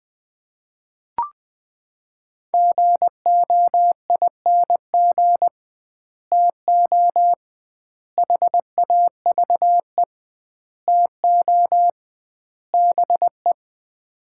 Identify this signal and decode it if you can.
Morse code, heard as a keyed tone, about 20 wpm, 700 Hz. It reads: GOING TO HAVE TO BE
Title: 2x Word-Spacing - Sets of 5 Words - 20wpm